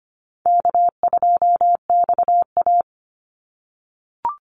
Morse code, heard as a keyed tone, about 25 wpm, 700 Hz. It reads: K2XA
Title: Easy Call Signs - 25wpm